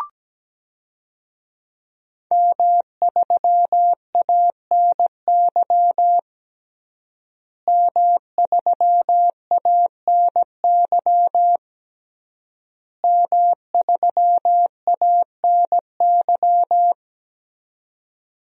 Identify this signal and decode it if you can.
Morse code, heard as a keyed tone, about 17 wpm, 700 Hz. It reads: M3ANY M3ANY M3ANY
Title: DXCC-100 Call Signs - 5 Characters - Repeated 3x - 17wpm